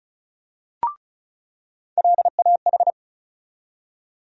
Morse code, heard as a keyed tone, about 36 wpm, 700 Hz. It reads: LAH
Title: USA Call Sign Postfixes - 3 Letters 35wpm